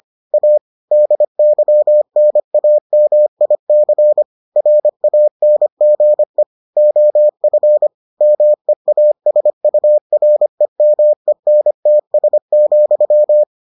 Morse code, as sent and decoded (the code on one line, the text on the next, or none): .- / -.. -.-- -. .- -- .. -.-. / .-. .- -. --. / --- ..-. / -- . .- ... ..- .-. -- . -. - ... --..--
A DYNAMIC RANGE OF MEASUREMENTS,